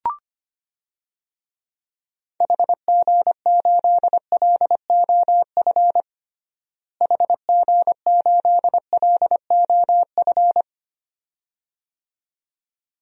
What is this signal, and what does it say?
Morse code, heard as a keyed tone, about 25 wpm, 700 Hz. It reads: HG8LOF HG8LOF
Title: DXCC-100 Call Signs - 6 Characters - Repeated 2x - 25wpm